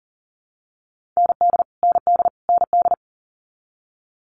Morse code, heard as a keyed tone, about 39 wpm, 690 Hz. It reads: ND ND ND
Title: Multi-letter Phonemes Repeated 3x 40wpm